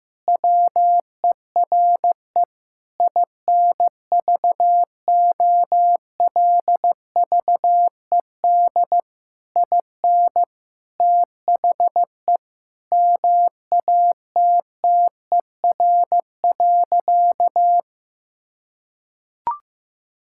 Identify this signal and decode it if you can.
Morse code, heard as a keyed tone, about 15 wpm, 700 Hz. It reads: WERE INVOLVED IN THE MATTER.
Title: Sentences from Top 1000 Words - Review Part 5 - 15wpm